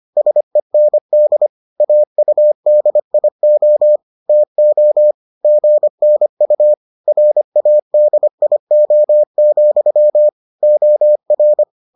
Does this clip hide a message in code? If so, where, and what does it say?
Yes: Morse code from 0.2 to 11.6 s, heard as a keyed tone, about 25 wpm, 600 Hz. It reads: SEND AUDIO TO GNU RADIO, OR